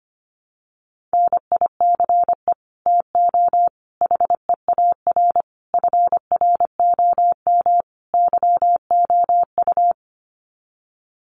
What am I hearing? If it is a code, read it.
Morse code, heard as a keyed tone, about 25 wpm, 700 Hz. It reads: NICE TO HEAR FROM YOU
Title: Common QSO Phrases 25wpm